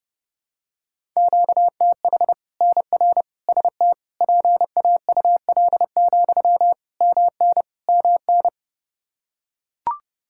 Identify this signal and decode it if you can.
Morse code, heard as a keyed tone, about 30 wpm, 700 Hz. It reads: QTH NR ST PAUL, MN MN
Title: Practice QSO - Exchange 1 - Beginning - 30wpm